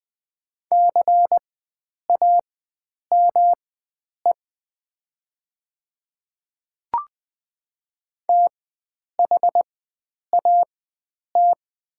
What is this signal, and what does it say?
Morse code, heard as a keyed tone, about 20 wpm, spaced as for 9 wpm, 700 Hz. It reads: CAME THAT
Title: 4x Character Spacing - Top 300 Words - 20wpm